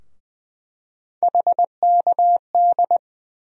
Morse code, heard as a keyed tone, about 20 wpm, 700 Hz. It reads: HKD
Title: USA Call Sign Postfixes - 3 Letters 20wpm